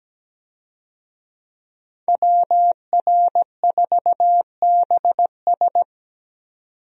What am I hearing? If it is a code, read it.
Morse code, heard as a keyed tone, about 17 wpm, 700 Hz. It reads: WR4BS
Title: USA Call Signs - 5 Characters - 17wpm